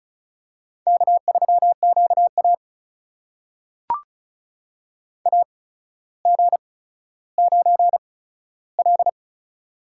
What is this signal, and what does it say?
Morse code, heard as a keyed tone, about 36 wpm, spaced as for 15 wpm, 700 Hz. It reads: K3QA AG9L
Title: USA Call Signs - 8x Character Spacing - 35wpm